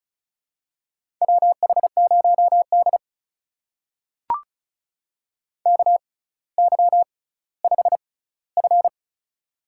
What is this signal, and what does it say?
Morse code, heard as a keyed tone, about 34 wpm, spaced as for 18 wpm, 700 Hz. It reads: WH0D KY5F
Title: USA Call Signs - 4 Characters - 6x Character Spacing - 35wpm